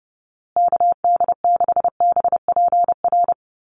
Morse code, heard as a keyed tone, about 30 wpm, 700 Hz. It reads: KD6BPR
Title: USA Call Signs Repeated 2x 30wpm